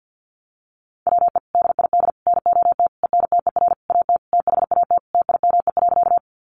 Morse code, heard as a keyed tone, about 50 wpm, 700 Hz. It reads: WE DID NOT ENTER AT THAT TIME.
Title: Sentences from Top 1300 Words 50wpm